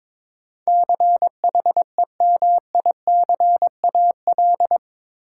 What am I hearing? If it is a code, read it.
Morse code, heard as a keyed tone, about 22 wpm, 700 Hz. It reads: CHEMICAL
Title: Sentences from Top 1200 Words - New Word Drill - 22wpm